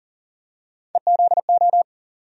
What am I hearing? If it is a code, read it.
Morse code, heard as a keyed tone, about 40 wpm, 700 Hz. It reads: EZO
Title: USA Call Sign Postfixes - 3 Letters 40wpm